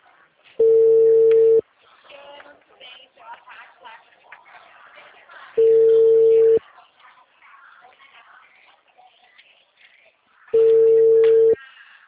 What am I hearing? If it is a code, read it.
Ringback tone (China, 450 Hz), with speech.
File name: out- -2014- - - .30498